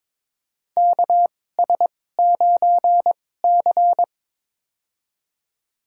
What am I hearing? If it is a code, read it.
Morse code, heard as a keyed tone, about 22 wpm, spaced as for 16 wpm, 700 Hz. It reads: KS9C
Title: USA Call Signs - 2x Character Spacing - 22wpm